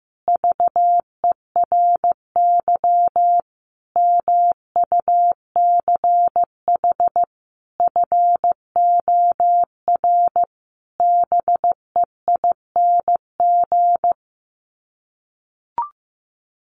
Morse code, heard as a keyed tone, about 15 wpm, 700 Hz. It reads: VERY MUCH FOR BEING
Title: Sets of 5 Words Encore 15wpm